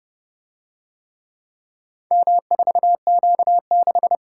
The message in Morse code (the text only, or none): M4Q6